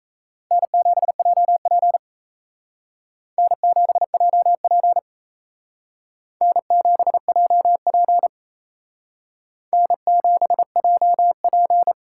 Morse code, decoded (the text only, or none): N7JP N7JP N7JP N7JP